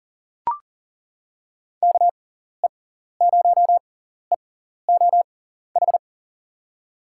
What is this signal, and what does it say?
Morse code, heard as a keyed tone, about 38 wpm, spaced as for 14 wpm, 700 Hz. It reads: KE0EOH
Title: USA Call Signs - 6 Characters - 6x Character Spacing - 40wpm